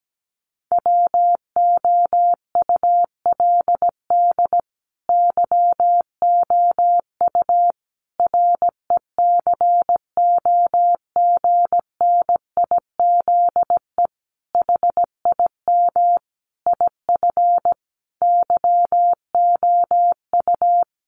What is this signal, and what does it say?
Morse code, heard as a keyed tone, about 17 wpm, 700 Hz. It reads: WOULD YOU RECOGNIZE HIM IF YOU
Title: Sentences from Top 1400 Words 17wpm